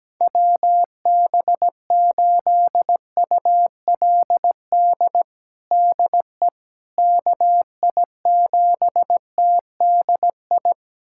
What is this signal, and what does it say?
Morse code, heard as a keyed tone, about 17 wpm, 700 Hz. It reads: WB8ULD DE KI7TDI
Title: USA Callsign DE Callsign - 6 Characters - 17wpm